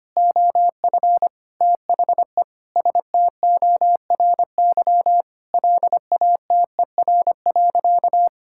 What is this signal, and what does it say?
Morse code, heard as a keyed tone, about 25 wpm, 700 Hz. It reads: OF THE STORY LATER.
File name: practice / 25wpm / Sentences with 4-Word Phrases - 25wpm